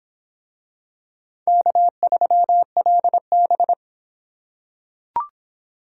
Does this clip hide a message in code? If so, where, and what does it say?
Yes: Morse code from 1.5 to 3.7 s, heard as a keyed tone, about 26 wpm, 700 Hz. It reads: K3LB